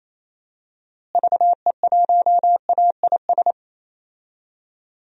Morse code, heard as a keyed tone, about 28 wpm, 700 Hz. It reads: VE1AIS
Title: DXCC-100 Call Signs - 6 Characters - Repeated 2x - 28wpm